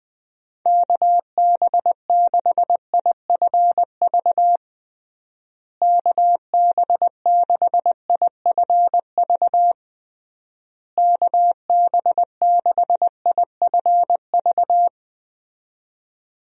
Morse code, heard as a keyed tone, about 21 wpm, 700 Hz. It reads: KB6IFV KB6IFV KB6IFV
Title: USA Call Signs - 6 Characters - Repeated 3x - 20wpm